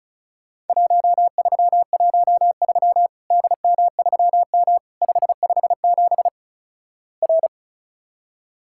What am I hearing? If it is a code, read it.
Morse code, heard as a keyed tone, about 36 wpm, 700 Hz. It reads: 1313 DM3M 557 R
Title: Worked All Europe DX Contest - QTC Exchange - 35wpm